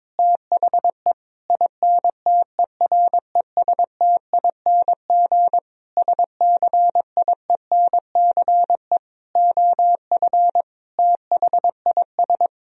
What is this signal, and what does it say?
Morse code, heard as a keyed tone, about 22 wpm, 700 Hz. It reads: THE INTERESTING SCIENCE OF THIS
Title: Sentences from Top 1000 Words - Review Part 2 - 22wpm